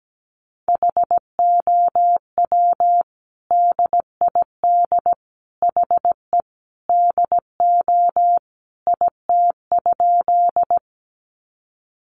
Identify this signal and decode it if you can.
Morse code, heard as a keyed tone, about 17 wpm, 700 Hz. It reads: HOW DID HE DO IT?
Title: Sentences from Top 100 Words 17wpm